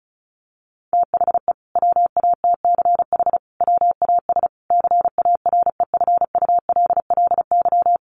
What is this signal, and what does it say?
Morse code, heard as a keyed tone, about 34 wpm, 700 Hz. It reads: THE WATCH WAS CAREFULLY
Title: Sentences from Top 1300 Words 35wpm